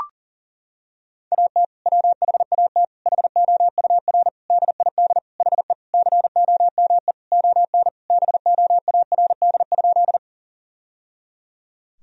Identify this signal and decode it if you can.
Morse code, heard as a keyed tone, about 40 wpm, 700 Hz. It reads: AT WHAT HOUR DID HE COME ON BOARD?
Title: Sentences from Top 1000 Words - Review Part 4 - 40wpm